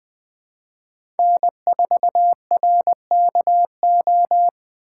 Morse code, heard as a keyed tone, about 20 wpm, 700 Hz. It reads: N4RKO
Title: USA Call Signs - 5 Characters - 3x Character Spacing - 20wpm